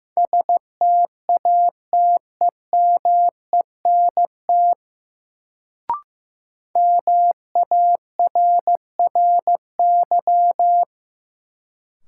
Morse code, heard as a keyed tone, about 15 wpm, 700 Hz. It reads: STATEMENT MARRY
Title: Sentences from Top 1200 Words - New Word Drill - 15wpm